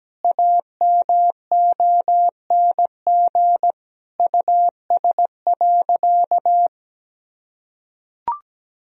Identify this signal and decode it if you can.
Morse code, heard as a keyed tone, about 17 wpm, 700 Hz. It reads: AMONG US.
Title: Sentences from Top 1000 Words - Review Part 3 - 17wpm